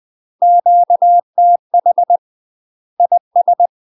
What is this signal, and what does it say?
Morse code, heard as a keyed tone, about 20 wpm, 700 Hz. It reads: QTH IS
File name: UK MASTER QSOs TRAINING / 20WPM / UK MASTER QSOs 0041